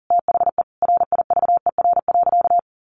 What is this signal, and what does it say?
Morse code, heard as a keyed tone, about 40 wpm, 700 Hz. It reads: THE RIVER.
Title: Sentences from Top 1300 Words 40wpm